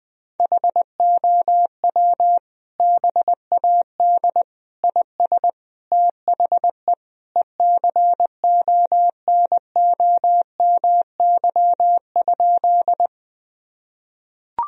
Morse code, heard as a keyed tone, about 21 wpm, 700 Hz. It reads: HOW BAD IS THE ECONOMY?
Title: Sentences from Top 1000 Words - Review Part 3 - 20wpm